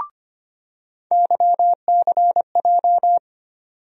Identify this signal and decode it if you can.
Morse code, heard as a keyed tone, about 25 wpm, 700 Hz. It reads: YCJ
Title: USA Call Sign Postfixes - 3 Letters 25wpm